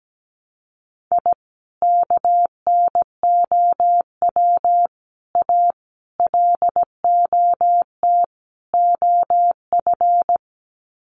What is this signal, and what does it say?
Morse code, heard as a keyed tone, about 17 wpm, 700 Hz. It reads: I KNOW A LOT OF
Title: Sets of 5 Words Encore 17wpm